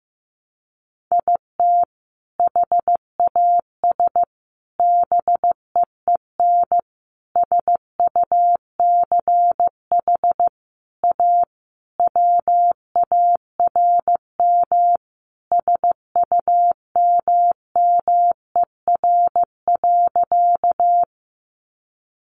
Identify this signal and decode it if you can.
Morse code, heard as a keyed tone, about 15 wpm, 700 Hz. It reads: IT HAS BEEN SUCH A WARM SUMMER.
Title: Sentences from Top 1100 Words 15wpm